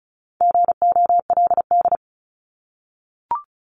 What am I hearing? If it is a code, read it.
Morse code, heard as a keyed tone, about 36 wpm, 700 Hz. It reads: GOLD
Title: Multi-letter Phonemes Repeated 3x 35wpm